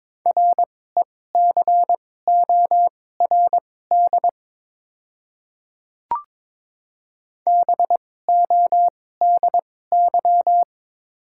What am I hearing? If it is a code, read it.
Morse code, heard as a keyed tone, about 22 wpm, spaced as for 16 wpm, 700 Hz. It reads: RECORD BODY
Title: 2x Character Spacing - Top 500 Words - 22wpm